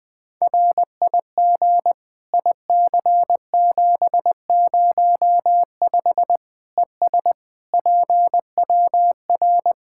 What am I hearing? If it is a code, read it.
Morse code, heard as a keyed tone, about 20 wpm, 700 Hz. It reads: RIG IC705 ES PWR
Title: Practice QSO - Exchange 2 - Info - 20wpm